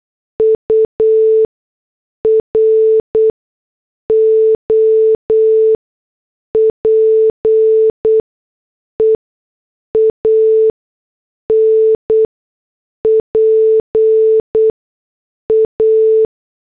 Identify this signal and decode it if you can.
Morse code, heard as a keyed tone, about 8 wpm, spaced as for 6 wpm, 440 Hz. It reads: UROPEAN PA
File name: morse / headlines